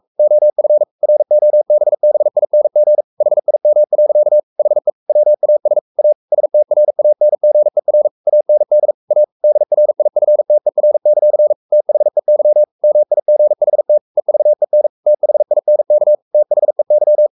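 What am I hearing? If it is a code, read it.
Morse code, heard as a keyed tone, about 44 wpm, 600 Hz. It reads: OF ROBBING HIM. HE WAS A STRANGER AND A DRIFTER; THEY MIGHT EVEN THINK THEY
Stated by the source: Morse code at 43 wpm sending Ken Follett's novel Never in English, from CW med Gnister.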